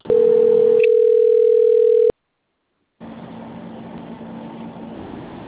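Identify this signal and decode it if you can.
Ringback tone (North America), 440 + 480 Hz.
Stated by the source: Call progress tone on an outbound telephone call.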